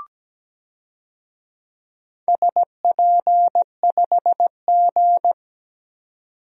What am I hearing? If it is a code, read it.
Morse code, heard as a keyed tone, about 17 wpm, 700 Hz. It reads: SP5G
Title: DXCC-100 Call Signs - 4 Characters - 17wpm